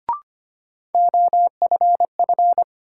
Morse code, heard as a keyed tone, about 25 wpm, 700 Hz. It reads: OFF
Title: Sets of 2 Words - New Word Drill - 25wpm